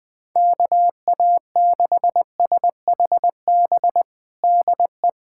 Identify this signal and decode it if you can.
Morse code, heard as a keyed tone, about 20 wpm, 700 Hz. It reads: KA6SHB DE
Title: USA Callsign DE Callsign 20wpm